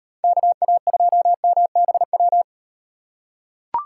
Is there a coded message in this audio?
Yes: Morse code from 0.2 to 2.4 s, heard as a keyed tone, about 38 wpm, 700 Hz. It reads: KA2MBW